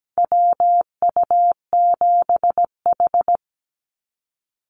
Morse code, heard as a keyed tone, about 17 wpm, 700 Hz. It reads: WU7H
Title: NA SOTA Activator Callsigns 17wpm